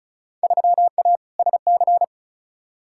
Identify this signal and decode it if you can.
Morse code, heard as a keyed tone, about 36 wpm, 700 Hz. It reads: 3A SC
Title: Field Day Exchanges Encore - 35wpm